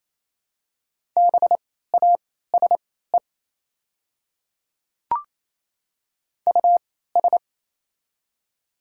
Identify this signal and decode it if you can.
Morse code, heard as a keyed tone, about 28 wpm, spaced as for 16 wpm, 690 Hz. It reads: BASE US
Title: 3x Character Spacing - Top 500 Words - 28wpm